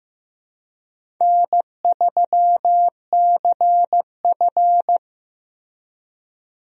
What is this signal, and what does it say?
Morse code, heard as a keyed tone, about 15 wpm, 700 Hz. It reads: N3CF
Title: USA Call Signs - 4 Characters - 15wpm